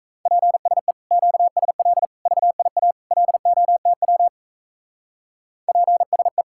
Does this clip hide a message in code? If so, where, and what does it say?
Yes: Morse code from 0.3 to 6.4 s, heard as a keyed tone, about 36 wpm, 700 Hz. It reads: PSE QSL VIA LOTW PSE